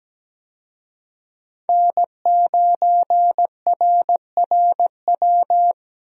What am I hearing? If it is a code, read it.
Morse code, heard as a keyed tone, about 17 wpm, 700 Hz. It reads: N9RRW
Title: USA Call Signs - 5 Characters - Repeated 2x - 17wpm